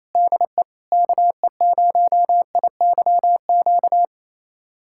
Morse code, heard as a keyed tone, about 27 wpm, 700 Hz. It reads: DE KE0IYQ
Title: USA Callsign DE Callsign - 6 Characters - 28wpm